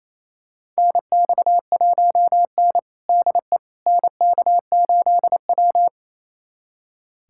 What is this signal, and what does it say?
Morse code, heard as a keyed tone, about 28 wpm, 690 Hz. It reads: NX1N DE NK8W
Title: USA Callsign DE Callsign - 4 Characters - 28wpm